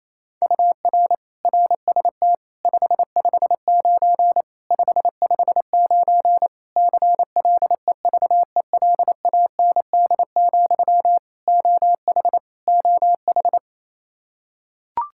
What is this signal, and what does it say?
Morse code, heard as a keyed tone, about 27 wpm, 700 Hz. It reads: UR RST 559 559 CLEVELAND, OH OH
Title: K1USN SST QSO Practice 28wpm